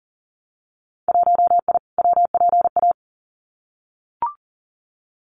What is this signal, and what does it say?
Morse code, heard as a keyed tone, about 37 wpm, 700 Hz. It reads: 1I WPA
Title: Winter Field Day Exchanges Encore 40wpm